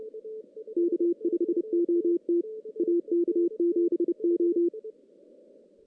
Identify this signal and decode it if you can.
Morse code, heard as a keyed tone, about 30 wpm, 340 Hz, with radio noise. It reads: K5OT AK7O
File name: 2016 JAN CW NAQP / 2016 Jan naqp k5ot - 0-6